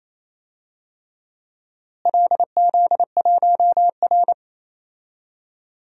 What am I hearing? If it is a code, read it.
Morse code, heard as a keyed tone, about 27 wpm, 700 Hz. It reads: LZ1R
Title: DXCC-100 Call Signs - 4 Characters - Repeated 3x - 28wpm